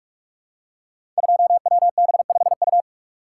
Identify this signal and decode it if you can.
Morse code, heard as a keyed tone, about 44 wpm, 700 Hz. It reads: 2WB5U
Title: Code Groups - 5 Letter:Numbers - 45wpm